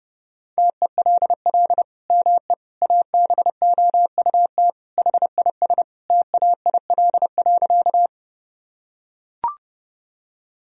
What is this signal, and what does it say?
Morse code, heard as a keyed tone, about 30 wpm, 700 Hz. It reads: TELL ME ABOUT HIS TAIL.
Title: Sentences from Top 1000 Words - Review Part 3 - 30wpm